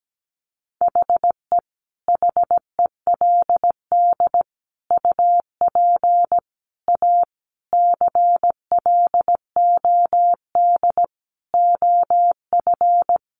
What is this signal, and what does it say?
Morse code, heard as a keyed tone, about 17 wpm, 700 Hz. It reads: HE HELD UP A CLOD OF
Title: A Sound Of Thunder 17wpm